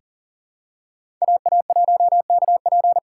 Morse code, heard as a keyed tone, about 40 wpm, 700 Hz. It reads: AA1KP
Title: USA Call Signs - 6x Character Spacing - 40wpm